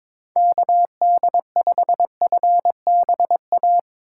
Morse code, heard as a keyed tone, about 22 wpm, 700 Hz. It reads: KD5FBA
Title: K1USN SST QSO Practice - 22wpm